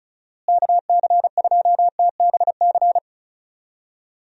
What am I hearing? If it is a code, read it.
Morse code, heard as a keyed tone, about 34 wpm, 700 Hz. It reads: KC2TBC
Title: USA Call Signs - 6 Characters - Repeated 3x - 35wpm